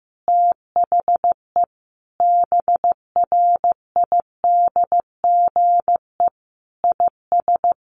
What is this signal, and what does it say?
Morse code, heard as a keyed tone, about 15 wpm, 700 Hz. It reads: THE BRIDGE IS
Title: Sentences from Top 1400 Words 15wpm